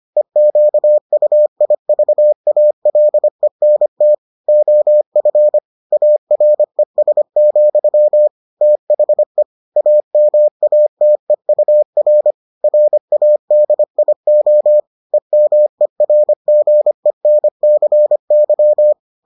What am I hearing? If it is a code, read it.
Morse code, heard as a keyed tone, about 25 wpm, 600 Hz. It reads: EQUIVALENT OF ARES, THE AMATEUR RADIO EMERGENCY